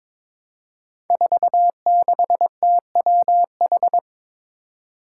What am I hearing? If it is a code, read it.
Morse code, heard as a keyed tone, about 21 wpm, 700 Hz. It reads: 46TWH
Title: Five Challenging Letter-Number Combinations 22wpm